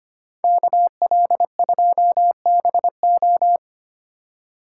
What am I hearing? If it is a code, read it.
Morse code, heard as a keyed tone, about 29 wpm, 700 Hz. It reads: KL2BO
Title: Speed Racing - USA 2x2 Call Signs 25wpm